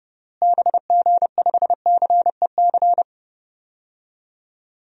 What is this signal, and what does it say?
Morse code, heard as a keyed tone, about 32 wpm, 700 Hz. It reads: BG5CEC
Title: DXCC-100 Call Signs - 6 Characters - Repeated 3x - 30wpm